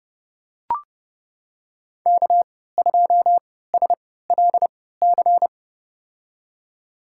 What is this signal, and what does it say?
Morse code, heard as a keyed tone, about 30 wpm, spaced as for 17 wpm, 700 Hz. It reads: K2SLC
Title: USA Call Signs - 5 Characters - 3x Character Spacing - 30wpm